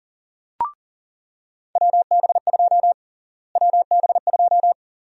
Morse code, heard as a keyed tone, about 40 wpm, 700 Hz. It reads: WB2 WB2
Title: USA Call Sign Prefixes - Repeated 2x - 40wpm